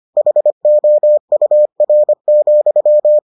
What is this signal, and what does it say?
Morse code, heard as a keyed tone, about 25 wpm, 600 Hz. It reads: HOUR,